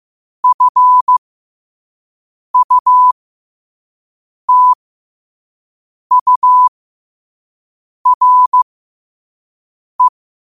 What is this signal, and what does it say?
Morse code, heard as a keyed tone, about 15 wpm, spaced as for 5 wpm, 1000 Hz. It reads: FUTURE
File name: qotdfiles / 5wpm